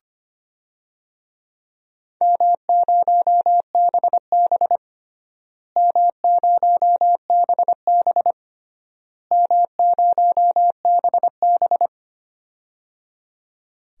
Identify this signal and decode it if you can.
Morse code, heard as a keyed tone, about 25 wpm, 700 Hz. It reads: M0BB M0BB M0BB
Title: DXCC-100 Call Signs - 4 Characters - Repeated 3x - 25wpm